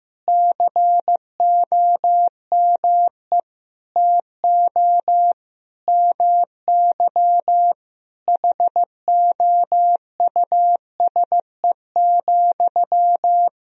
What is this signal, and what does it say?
Morse code, heard as a keyed tone, about 15 wpm, 700 Hz. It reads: COME TO MY HOUSE,
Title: Sentences from Top 400 Words and Abbreviations 15wpm